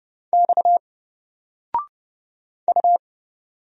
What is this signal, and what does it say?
Morse code, heard as a keyed tone, about 30 wpm, 700 Hz. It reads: X U
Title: Single Letter-Number Alt 30wpm